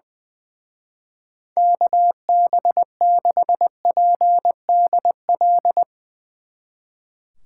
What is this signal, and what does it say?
Morse code, heard as a keyed tone, about 20 wpm, 700 Hz. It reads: KB6PDL
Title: Speed Racing - USA Call Signs 20wpm